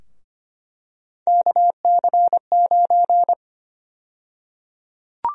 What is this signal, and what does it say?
Morse code, heard as a keyed tone, about 25 wpm, 700 Hz. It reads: KC9
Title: USA Call Sign Prefixes - Repeated 2x - 25wpm